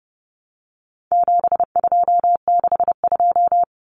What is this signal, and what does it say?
Morse code, heard as a keyed tone, about 30 wpm, 700 Hz. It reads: 7262
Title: Common Cut Numbers - 4-Digit 30wpm